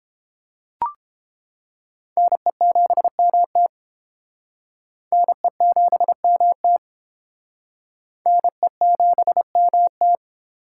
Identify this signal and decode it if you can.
Morse code, heard as a keyed tone, about 29 wpm, 700 Hz. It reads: NE7MT NE7MT NE7MT